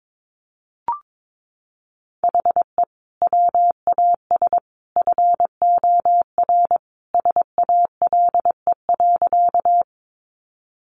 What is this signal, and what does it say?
Morse code, heard as a keyed tone, about 22 wpm, 700 Hz. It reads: HE WAS FOR SALE.
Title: Sentences from Top 1400 Words 22wpm